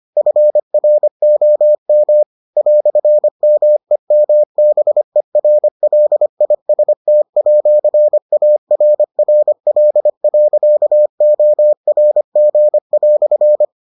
Morse code, heard as a keyed tone, about 25 wpm, 600 Hz. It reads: FROM "MEMBERLIST@ARRL.ORG"